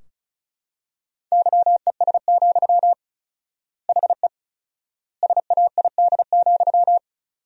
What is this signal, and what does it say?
Morse code, heard as a keyed tone, about 36 wpm, 700 Hz. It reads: YES, HE SAID,
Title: 4x Word-Spacing - Sentences from Top 500 Words - 35wpm